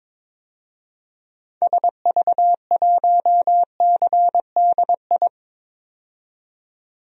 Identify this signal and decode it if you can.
Morse code, heard as a keyed tone, about 22 wpm, 700 Hz. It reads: SV1CDI
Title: DXCC-100 Call Signs - 6 Characters - 22wpm